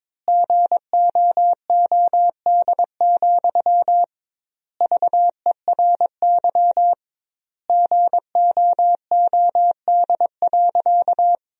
Morse code, heard as a keyed tone, about 22 wpm, 700 Hz. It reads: GOOD, VERY GOOD.
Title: 2x Word-Spacing - Sentences from Top 200 Words - 22wpm